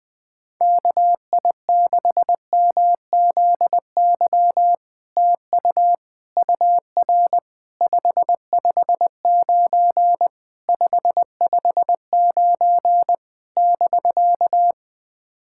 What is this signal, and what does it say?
Morse code, heard as a keyed tone, about 20 wpm, 700 Hz. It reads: KI6MZY TU UR 559 559 <BK>
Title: SOTA QSO Practice 20wpm